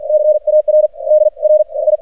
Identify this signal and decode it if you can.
Morse code, heard as a keyed tone, about 44 wpm, spaced as for 34 wpm, 610 Hz. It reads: CAR DRI
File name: archived / CRSnet